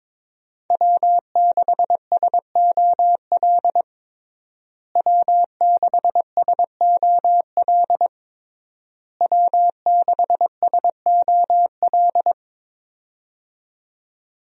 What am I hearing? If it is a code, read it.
Morse code, heard as a keyed tone, about 22 wpm, 700 Hz. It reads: W6SOL W6SOL W6SOL
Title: USA Call Signs - 5 Characters - Repeated 3x - 22wpm